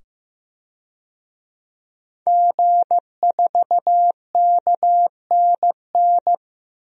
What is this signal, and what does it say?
Morse code, heard as a keyed tone, about 15 wpm, 700 Hz. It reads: G4KNN
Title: DXCC-100 Call Signs - 5 Characters - 15wpm